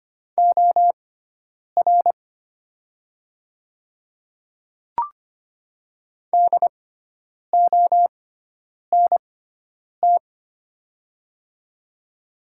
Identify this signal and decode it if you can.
Morse code, heard as a keyed tone, about 25 wpm, spaced as for 9 wpm, 700 Hz. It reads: OR DONT
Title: 6x Character Spacing - Top 300 Words - 25wpm